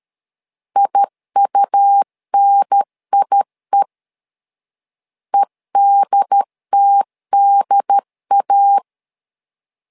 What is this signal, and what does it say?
Morse code, heard as a keyed tone, about 13 wpm, 800 Hz. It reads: IUNIE EDTDA
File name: LEZIONI / 2 UHDN / 2 UHDN 12wpm 1456